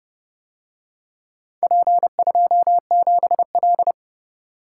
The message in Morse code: P27L

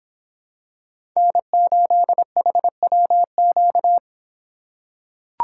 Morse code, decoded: N8HWQ

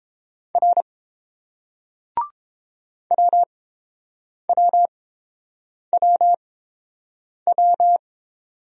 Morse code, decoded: R WWWW

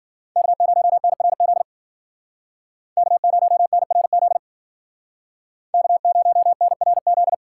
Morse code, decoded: K0NRZ K0NRZ K0NRZ

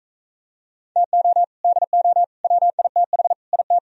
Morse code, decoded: TO DO WITH IT

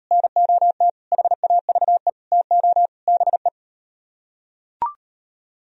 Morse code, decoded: NOT HAVE TO BE

